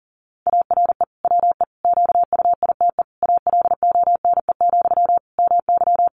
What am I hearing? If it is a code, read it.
Morse code, heard as a keyed tone, about 37 wpm, 700 Hz. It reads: ARE WE QUITE ALONE, MY
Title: Sentences from Top 1000 Words 40wpm